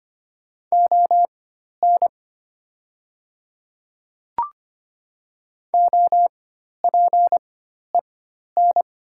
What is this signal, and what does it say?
Morse code, heard as a keyed tone, about 25 wpm, spaced as for 12 wpm, 700 Hz. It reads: ON OPEN
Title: 4x Character Spacing - Top 500 Words - 25wpm